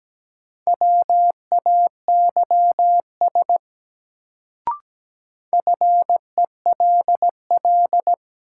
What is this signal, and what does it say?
Morse code, heard as a keyed tone, about 17 wpm, 700 Hz. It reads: WAYS FELL
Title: Sets of 5 Words Encore - New Word Drill - 17wpm